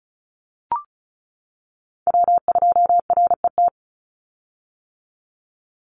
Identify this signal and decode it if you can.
Morse code, heard as a keyed tone, about 34 wpm, 700 Hz. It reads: W2RET
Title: USA Call Signs 35wpm